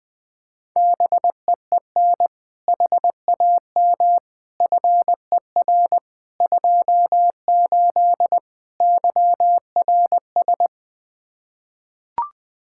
Morse code, heard as a keyed tone, about 20 wpm, 700 Hz. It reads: BEEN HAM FER 28 YRS
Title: Practice QSO - Exchange 3 - More Info - 20wpm